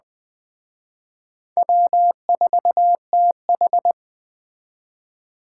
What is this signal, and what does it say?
Morse code, heard as a keyed tone, about 20 wpm, 700 Hz. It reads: W4TH
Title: Speed Racing - USA 1x2 Call Signs 17wpm